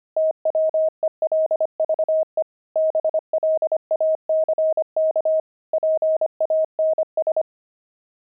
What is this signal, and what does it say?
Morse code, heard as a keyed tone, about 25 wpm, 630 Hz. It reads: TWELVE BLACK PANS
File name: pangram-fs-25-cs-25-twelve